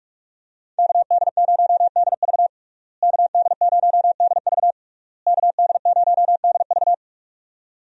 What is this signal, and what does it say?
Morse code, heard as a keyed tone, about 41 wpm, 700 Hz. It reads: KD0DV KD0DV KD0DV
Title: USA Call Signs - 5 Characters - Repeated 3x - 45wpm